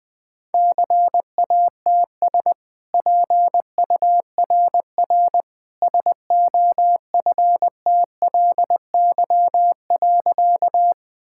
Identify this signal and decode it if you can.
Morse code, heard as a keyed tone, about 20 wpm, 700 Hz. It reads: CATS PURR SOFTLY.
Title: Easy Sentences with Hints - 20wpm